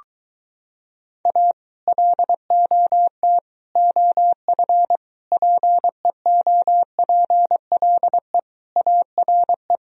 Morse code, heard as a keyed tone, about 23 wpm, 700 Hz. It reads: A LOT OF PEOPLE ARE